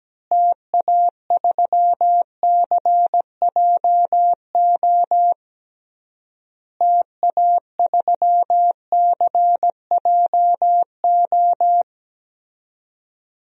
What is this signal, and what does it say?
Morse code, heard as a keyed tone, about 17 wpm, 700 Hz. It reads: TA3CJO TA3CJO
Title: DXCC-100 Call Signs - 6 Characters - Repeated 3x - 17wpm